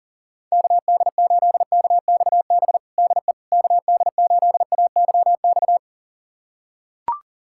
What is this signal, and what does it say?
Morse code, heard as a keyed tone, about 40 wpm, 700 Hz. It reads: KD8KXB DE KD8AYX
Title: USA Callsign DE Callsign 40wpm